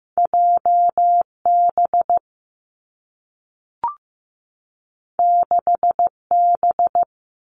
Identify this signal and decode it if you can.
Morse code, heard as a keyed tone, about 15 wpm, 700 Hz. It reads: J B 6 B
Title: Two Challenging Letter-Number Combinations 15wpm